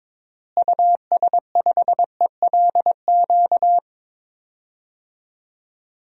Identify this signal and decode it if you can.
Morse code, heard as a keyed tone, about 22 wpm, 700 Hz. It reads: US5ELQ
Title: DXCC-100 Call Signs - 6 Characters - Repeated 2x - 22wpm